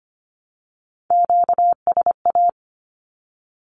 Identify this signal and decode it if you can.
Morse code, heard as a keyed tone, about 25 wpm, 700 Hz. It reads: QSA
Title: Q-Codes 25wpm